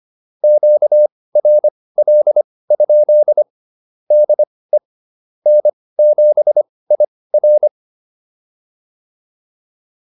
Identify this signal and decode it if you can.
Morse code, heard as a keyed tone, about 25 wpm, spaced as for 18 wpm, 600 Hz. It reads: QRL? DE N7IR